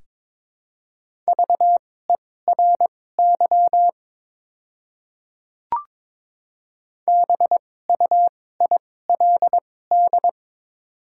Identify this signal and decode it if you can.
Morse code, heard as a keyed tone, about 23 wpm, spaced as for 16 wpm, 700 Hz. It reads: VERY BUILD
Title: 2x Character Spacing - Top 200 Words - 22wpm